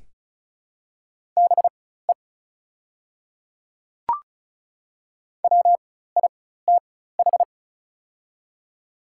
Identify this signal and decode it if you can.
Morse code, heard as a keyed tone, about 36 wpm, spaced as for 16 wpm, 700 Hz. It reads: BE WITH
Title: 4x Character Spacing - Top 200 Words - 35wpm